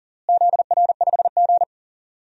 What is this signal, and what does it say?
Morse code, heard as a keyed tone, about 40 wpm, 700 Hz. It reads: ZR5G
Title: DXCC-100 Call Signs - 4 Characters - Repeated 3x - 40wpm